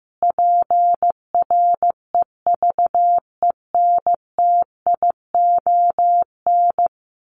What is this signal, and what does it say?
Morse code, heard as a keyed tone, about 15 wpm, 700 Hz. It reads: PREVENTION
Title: Sets of 5 Words Encore2 15wpm